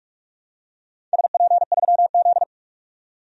Morse code, heard as a keyed tone, about 45 wpm, 700 Hz. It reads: SP3Z
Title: DXCC-100 Call Signs - 4 Characters - 45wpm